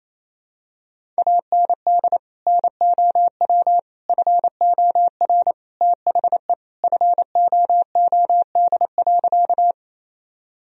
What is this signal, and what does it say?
Morse code, heard as a keyed tone, about 28 wpm, 700 Hz. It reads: AND NOW FOR THE FOOD.
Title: Sentences from Top 1000 Words - Review Part 1 - 28wpm